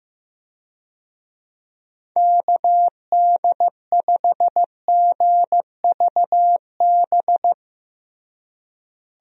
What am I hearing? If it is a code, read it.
Morse code, heard as a keyed tone, about 16 wpm, 700 Hz. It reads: KD5GVB
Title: USA Call Signs - 6 Characters - Repeated 2x - 15wpm